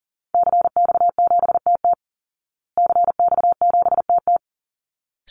Morse code, heard as a keyed tone, about 40 wpm, 700 Hz. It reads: CX7TT CX7TT
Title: Call Signs Repeated 3x - 40wpm